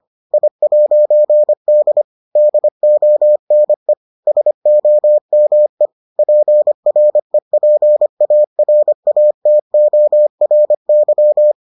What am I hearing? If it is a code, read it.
Morse code, heard as a keyed tone, about 25 wpm, 600 Hz. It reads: I'D DONE SOME PREPARATORY